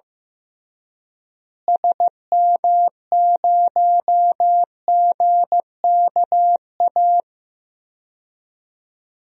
Morse code, heard as a keyed tone, about 15 wpm, 700 Hz. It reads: SM0GKA
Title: DXCC-100 Call Signs - 6 Characters - Repeated 3x - 15wpm